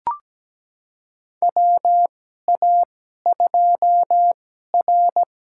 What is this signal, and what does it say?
Morse code, heard as a keyed tone, about 17 wpm, spaced as for 12 wpm, 700 Hz. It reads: WA2R
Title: USA Call Signs - 2x Character Spacing - 17wpm